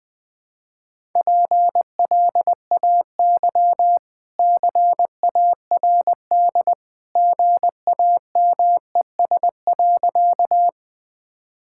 Morse code, heard as a keyed tone, about 21 wpm, 700 Hz. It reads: PLAY CARD GAMES.
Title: Easy Sentences with Hints - 20wpm